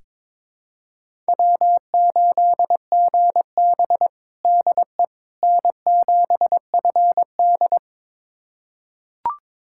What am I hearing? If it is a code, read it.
Morse code, heard as a keyed tone, about 22 wpm, 700 Hz. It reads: W8GB DE N7FD
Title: USA Callsign DE Callsign - 4 Characters - 22wpm